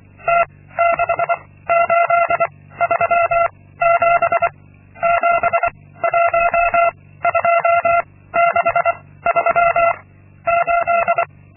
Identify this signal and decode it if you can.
Morse code, heard as a keyed tone, about 25 wpm, 680 Hz. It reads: T 6 8 3 7 7 1 2 6 3 8
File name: Linksmakalnis-2